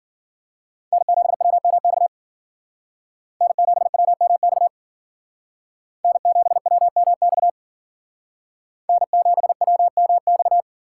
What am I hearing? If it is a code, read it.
Morse code, heard as a keyed tone, about 52 wpm, 700 Hz. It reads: N7WMX N7WMX N7WMX N7WMX